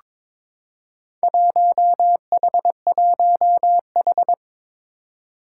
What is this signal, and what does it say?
Morse code, heard as a keyed tone, about 21 wpm, 700 Hz. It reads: 1H1H
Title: Four Challenging Letter-Number Combinations 22wpm